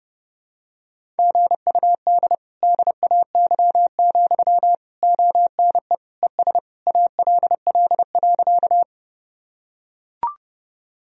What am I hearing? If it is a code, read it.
Morse code, heard as a keyed tone, about 30 wpm, 700 Hz. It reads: GUD DAY, ONE ES ALL.
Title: Sentences from Top 200 Words and Abbreviations 30wpm